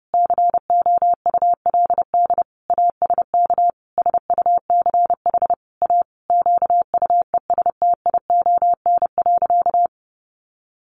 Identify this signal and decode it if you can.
Morse code, heard as a keyed tone, about 30 wpm, 700 Hz. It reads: COULD ASK SUCH A QUESTION.
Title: Sentences from Top 600 Words 30wpm